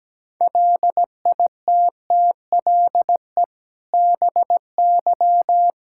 Morse code, heard as a keyed tone, about 17 wpm, 700 Hz. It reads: LITTLE BY
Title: Sentences from Top 1600 Words 17wpm